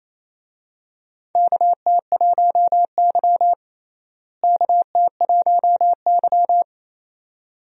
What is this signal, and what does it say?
Morse code, heard as a keyed tone, about 28 wpm, 700 Hz. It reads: KT1Y KT1Y
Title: USA Call Signs - 4 Characters - Repeated 2x - 28wpm